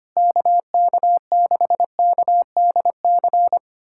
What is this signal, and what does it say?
Morse code, heard as a keyed tone, about 25 wpm, 700 Hz. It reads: KK6KDC
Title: USA Callsign DE Callsign - 6 Characters - 25wpm